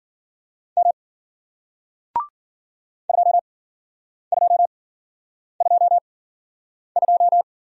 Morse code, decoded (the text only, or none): M 2222